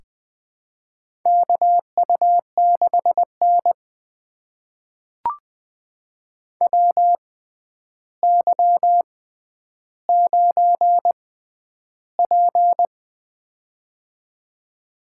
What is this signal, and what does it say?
Morse code, heard as a keyed tone, about 20 wpm, spaced as for 10 wpm, 700 Hz. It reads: KU6N WY9P
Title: USA Call Signs - 4 Characters - 6x Character Spacing - 20wpm